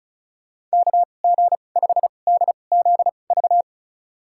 Morse code, heard as a keyed tone, about 36 wpm, spaced as for 25 wpm, 700 Hz. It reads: KG5DZV
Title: USA Call Signs - 6 Characters - 2x Character Spacing - 35wpm